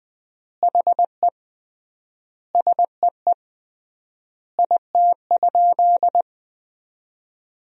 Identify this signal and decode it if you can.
Morse code, heard as a keyed tone, about 20 wpm, 700 Hz. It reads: HE SEE IT?
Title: 3x Word-Spacing - Sentences from Top 100 Words - 20wpm